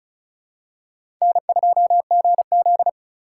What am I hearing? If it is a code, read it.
Morse code, heard as a keyed tone, about 36 wpm, 700 Hz. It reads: N2GZ
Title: USA Call Signs - 2x Character Spacing - 35wpm